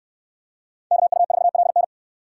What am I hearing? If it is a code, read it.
Morse code, heard as a keyed tone, about 64 wpm, 700 Hz. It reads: KU4LN